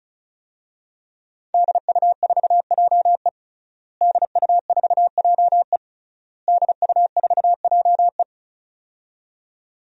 Morse code, heard as a keyed tone, about 34 wpm, 700 Hz. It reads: DU4JE DU4JE DU4JE
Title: DXCC-100 Call Signs - 5 Characters - Repeated 3x - 35wpm